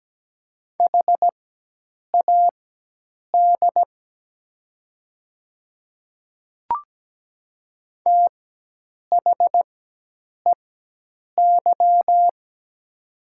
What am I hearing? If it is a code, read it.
Morse code, heard as a keyed tone, about 17 wpm, spaced as for 8 wpm, 700 Hz. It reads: HAD THEY
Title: 4x Character Spacing - Top 100 Words - 17wpm